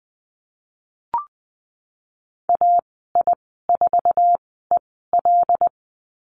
Morse code, heard as a keyed tone, about 20 wpm, spaced as for 15 wpm, 700 Hz. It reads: AI4EL
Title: 088 Lesson 15g - Number 5 - Call Signs 20wpm